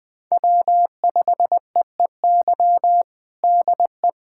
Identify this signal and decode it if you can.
Morse code, heard as a keyed tone, about 20 wpm, 700 Hz. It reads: W5EEY DE
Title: USA Callsign DE Callsign - 5 Characters - 20wpm